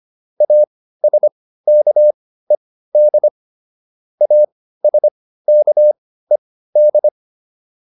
Morse code, heard as a keyed tone, about 25 wpm, spaced as for 15 wpm, 600 Hz. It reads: ASKED ASKED